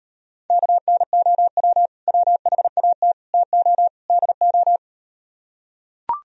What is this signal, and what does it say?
Morse code, heard as a keyed tone, about 38 wpm, 700 Hz. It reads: KNOW WHAT TO DO